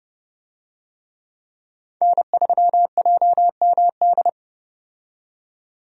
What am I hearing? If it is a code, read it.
Morse code, heard as a keyed tone, about 30 wpm, 700 Hz. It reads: N3JMD
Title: USA Call Signs - 5 Characters - 30wpm